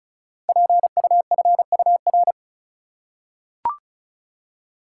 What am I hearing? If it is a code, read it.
Morse code, heard as a keyed tone, about 36 wpm, 700 Hz. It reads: PUFUR
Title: Code Groups - 5 Letters - 35wpm